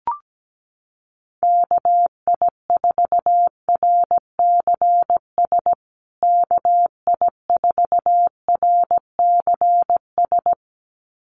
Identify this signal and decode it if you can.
Morse code, heard as a keyed tone, about 17 wpm, 700 Hz. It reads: KI4RCS KI4RCS
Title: USA Call Signs Repeated 2x 17wpm